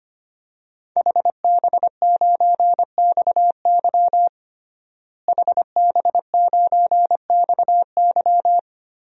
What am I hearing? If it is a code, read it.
Morse code, heard as a keyed tone, about 25 wpm, 700 Hz. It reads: HB9XY HB9XY
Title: DXCC-100 Call Signs - 5 Characters - Repeated 3x - 25wpm